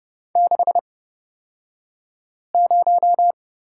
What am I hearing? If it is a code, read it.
Morse code, heard as a keyed tone, about 29 wpm, spaced as for 5 wpm, 700 Hz. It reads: 60
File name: Lesson 37d - Letter x - Two Characters 30wpm